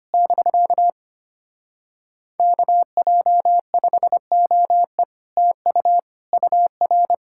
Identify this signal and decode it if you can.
Morse code, heard as a keyed tone, about 25 wpm, 700 Hz. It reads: <BK> KJ5OE TU UR
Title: Speed-Racing Common QSO Phrases - 25wpm